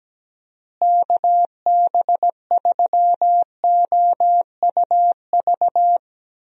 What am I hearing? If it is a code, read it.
Morse code, heard as a keyed tone, about 17 wpm, 700 Hz. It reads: KB3OUV
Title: USA Call Signs - 6 Characters - Repeated 2x - 17wpm